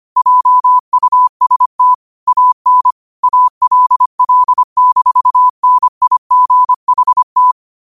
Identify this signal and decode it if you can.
Morse code, heard as a keyed tone, about 25 wpm, 1000 Hz. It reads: JUST AN ALL-NIGHT
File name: qotdfiles / 25wpm